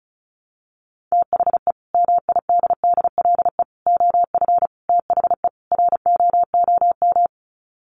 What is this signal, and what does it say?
Morse code, heard as a keyed tone, about 34 wpm, 700 Hz. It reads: THE MIDDLE OF THE ROOM
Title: Sets of 5 Words Encore2 35wpm